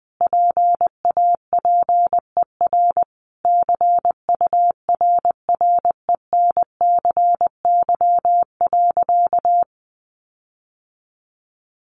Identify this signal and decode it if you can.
Morse code, heard as a keyed tone, about 20 wpm, 700 Hz. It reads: PAPER CURRENCY.
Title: Sentences from Top 1500 Words 20wpm